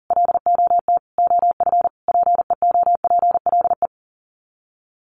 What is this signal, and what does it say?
Morse code, heard as a keyed tone, about 40 wpm, 700 Hz. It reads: LOT OF PEOPLE
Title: Sets of 5 Words Encore 40wpm